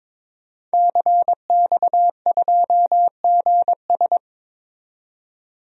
Morse code, heard as a keyed tone, about 22 wpm, 700 Hz. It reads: CX2GS
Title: DXCC-100 Call Signs - 5 Characters - 22wpm